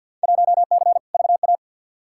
Morse code, heard as a keyed tone, about 49 wpm, 700 Hz. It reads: 1C VA
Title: Field Day Exchanges Encore - 50wpm